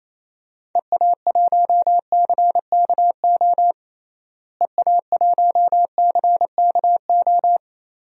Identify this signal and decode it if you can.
Morse code, heard as a keyed tone, about 28 wpm, 700 Hz. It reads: EA1CKO EA1CKO
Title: DXCC-100 Call Signs - 6 Characters - Repeated 2x - 28wpm